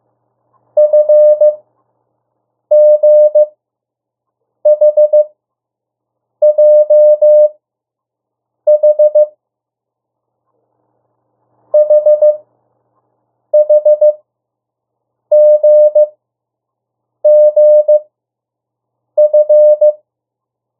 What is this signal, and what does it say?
Morse code, heard as a keyed tone, about 15 wpm, spaced as for 6 wpm, 600 Hz. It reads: FGHJH HHGGF
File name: morse mp3 / 02 urok 1